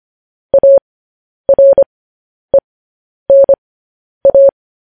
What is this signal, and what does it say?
Morse code, heard as a keyed tone, about 25 wpm, spaced as for 10 wpm, 550 Hz. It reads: ARENA